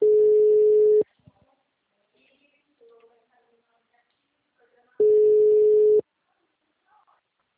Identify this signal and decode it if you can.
Ringback tone (Europe, 425 Hz), with speech.
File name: out- -2006- - - .1327